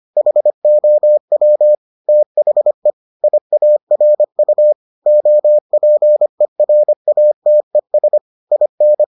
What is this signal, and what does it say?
Morse code, heard as a keyed tone, about 25 wpm, 600 Hz. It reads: HOW THE IARU OPERATES IN